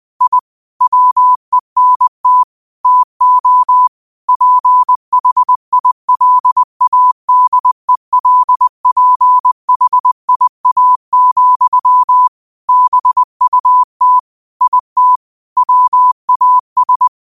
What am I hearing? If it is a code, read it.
Morse code, heard as a keyed tone, about 20 wpm, 1000 Hz. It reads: I WENT TO PHILADELPHIA, BUT IT WAS
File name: qotdfiles / 20wpm